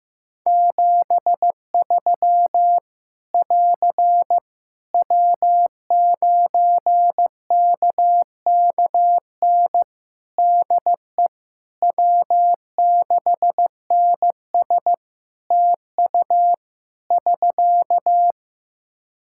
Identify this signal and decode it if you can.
Morse code, heard as a keyed tone, about 16 wpm, 700 Hz. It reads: 73 <AR> W9KKN DE W6NS TU <SK>
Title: Practice QSO - Exchange 4 - Ending - 15wpm